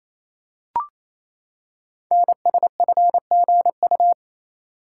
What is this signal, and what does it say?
Morse code, heard as a keyed tone, about 28 wpm, 700 Hz. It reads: NSFGU
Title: Code Groups - 5 Letters - 28wpm